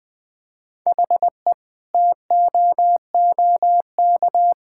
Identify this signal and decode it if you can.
Morse code, heard as a keyed tone, about 20 wpm, 700 Hz. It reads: HE TOOK
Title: Sentences from Top 1000 Words - Review Part 6 - 20wpm